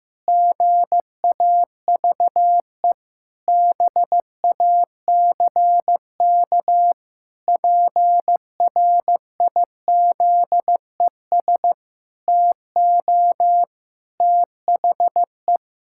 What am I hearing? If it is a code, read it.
Morse code, heard as a keyed tone, about 15 wpm, 700 Hz. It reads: GAVE BACK PRIZES TO THE
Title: Pangram Challenge v2 - 15wpm